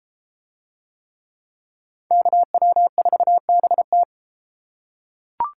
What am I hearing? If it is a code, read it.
Morse code, heard as a keyed tone, about 30 wpm, 700 Hz. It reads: KW4BT